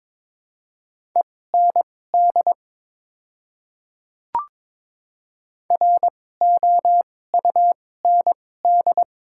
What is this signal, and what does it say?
Morse code, heard as a keyed tone, about 22 wpm, spaced as for 16 wpm, 700 Hz. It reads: END ROUND
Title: 2x Character Spacing - Top 200 Words - 22wpm